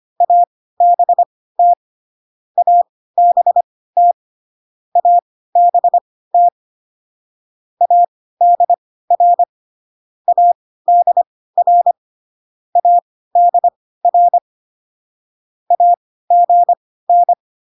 Morse code, heard as a keyed tone, about 25 wpm, spaced as for 16 wpm, 700 Hz. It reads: ABT ABT ABT ADR ADR ADR AGN